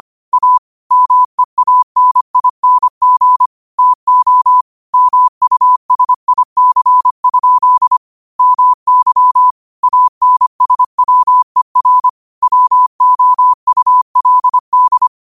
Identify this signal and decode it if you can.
Morse code, heard as a keyed tone, about 25 wpm, 1000 Hz. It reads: A MEANING TO MUSIC? MY ANSWER WOULD
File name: qotdfiles / 25wpm